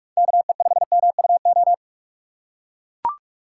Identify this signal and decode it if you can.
Morse code, heard as a keyed tone, about 43 wpm, 700 Hz. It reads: KE5MUO
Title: USA Call Signs - 6 Characters - 2x Character Spacing - 45wpm